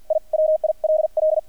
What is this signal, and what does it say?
Morse code, heard as a keyed tone, about 50 wpm, spaced as for 43 wpm, 620 Hz. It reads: I WILL
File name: archived / CRSnet